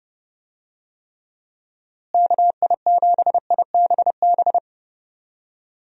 Morse code, heard as a keyed tone, about 30 wpm, 700 Hz. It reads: KI7IBB
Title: USA Call Signs - 6 Characters - Repeated 2x - 30wpm